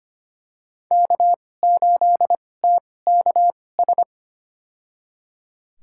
Morse code, heard as a keyed tone, about 25 wpm, spaced as for 18 wpm, 700 Hz. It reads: K8TKS
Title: USA Call Signs - 5 Characters - 2x Character Spacing - 25wpm